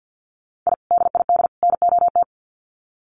Morse code, heard as a keyed tone, about 50 wpm, 700 Hz. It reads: I DID NOT
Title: Sets of 3 Words 50wpm